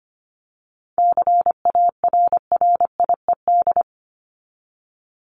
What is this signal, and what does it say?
Morse code, heard as a keyed tone, about 25 wpm, 700 Hz. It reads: CARRIED
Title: ED Suffix - Top 100 Words with Base Words 25wpm